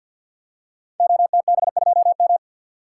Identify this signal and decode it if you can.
Morse code, heard as a keyed tone, about 48 wpm, 700 Hz. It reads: OTB2M
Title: Code Groups - 5 Letter:Numbers - 50wpm